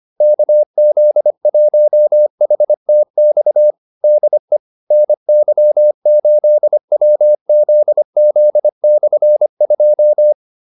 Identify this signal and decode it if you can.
Morse code, heard as a keyed tone, about 25 wpm, 600 Hz. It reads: KZ1HTX DE NY8WZZ/2